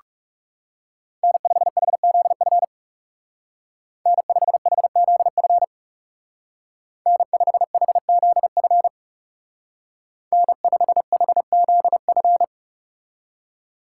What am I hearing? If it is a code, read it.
Morse code, heard as a keyed tone, about 37 wpm, 700 Hz. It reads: N5HZF N5HZF N5HZF N5HZF
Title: Speed Racing - USA 1x3 Call Signs 30wpm